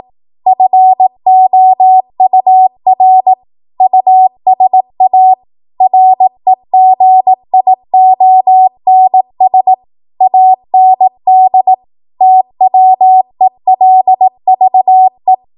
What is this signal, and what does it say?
Morse code, heard as a keyed tone, about 18 wpm, 750 Hz. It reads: FOUR USA REGIONS AND TWELVE
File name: Archive / 18 WPM / 18WPM